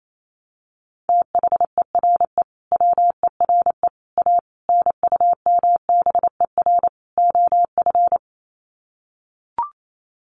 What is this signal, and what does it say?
Morse code, heard as a keyed tone, about 28 wpm, 700 Hz. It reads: THERE WERE A NUMBER OF
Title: Sets of 5 Words Encore2 28wpm